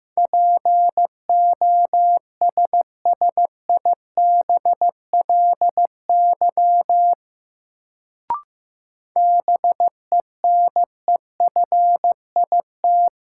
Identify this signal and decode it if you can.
Morse code, heard as a keyed tone, about 15 wpm, 700 Hz. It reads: POSSIBLY BENEFIT
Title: Sentences from Top 1300 Words - New Word Drill Alt - 15wpm